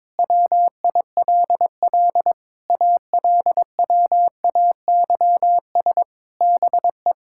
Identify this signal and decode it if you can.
Morse code, heard as a keyed tone, about 22 wpm, 700 Hz. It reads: WILL ALWAYS BE
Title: Sentences from Top 1000 Words - Review Part 6 - 22wpm